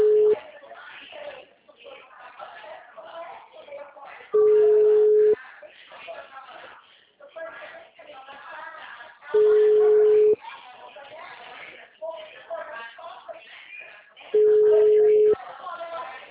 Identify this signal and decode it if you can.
Ringback tone (Europe, 425 Hz), with speech.
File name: out- -2013- - - .46357